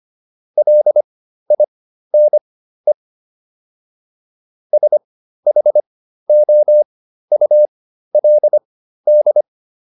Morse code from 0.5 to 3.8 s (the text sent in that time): LINE